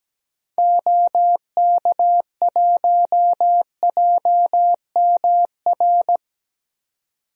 Morse code, heard as a keyed tone, about 17 wpm, 700 Hz. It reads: OK1JMR
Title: DXCC-100 Call Signs - 6 Characters - 17wpm